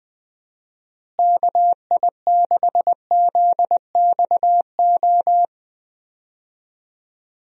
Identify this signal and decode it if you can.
Morse code, heard as a keyed tone, about 20 wpm, 700 Hz. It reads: KI6ZXO
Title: USA Call Signs - 6 Characters - Repeated 3x - 20wpm